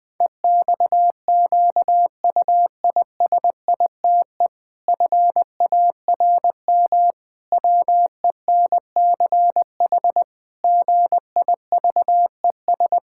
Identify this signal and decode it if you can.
Morse code, heard as a keyed tone, about 20 wpm, 700 Hz. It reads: EXQUISITE FARM WENCH GIVES
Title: Pangram Challenge v2 - 20wpm